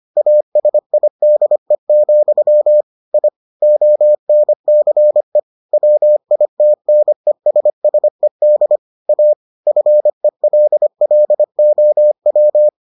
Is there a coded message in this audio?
Yes: Morse code from 0.2 to 12.7 s, heard as a keyed tone, about 25 wpm, 600 Hz. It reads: ASIDE, I ONCE WITNESSED A FELLOW